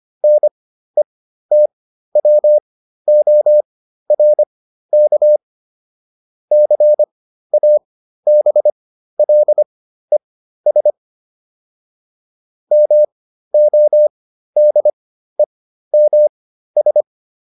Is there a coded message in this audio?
Yes: Morse code from 0.2 to 17.0 s, heard as a keyed tone, about 25 wpm, spaced as for 13 wpm, 600 Hz. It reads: NETWORK CABLES MODEMS